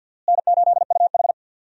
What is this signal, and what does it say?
Morse code, heard as a keyed tone, about 53 wpm, 700 Hz. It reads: N8UH
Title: USA Call Signs - 4 Characters - Repeated 2x - 50wpm